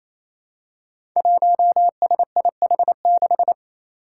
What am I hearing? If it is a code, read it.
Morse code, heard as a keyed tone, about 28 wpm, 700 Hz. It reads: 1SIH6
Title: Five Challenging Letter-Number Combinations 28wpm